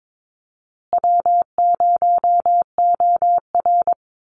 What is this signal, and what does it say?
Morse code, heard as a keyed tone, about 22 wpm, 700 Hz. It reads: W0OR
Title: Call Signs Repeated 3x - 22wpm